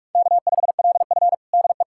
Morse code, heard as a keyed tone, about 46 wpm, 700 Hz. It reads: K5LF DE
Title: USA Callsign DE Callsign - 4 Characters - 45wpm